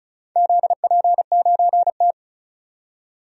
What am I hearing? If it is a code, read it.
Morse code, heard as a keyed tone, about 36 wpm, 690 Hz. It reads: ZP9T
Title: DXCC-100 Call Signs - 4 Characters - 35wpm